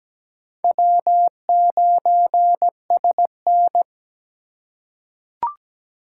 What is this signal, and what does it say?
Morse code, heard as a keyed tone, about 17 wpm, 700 Hz. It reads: W9SN
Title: USA Call Signs - 4 Characters - 2x Character Spacing - 17wpm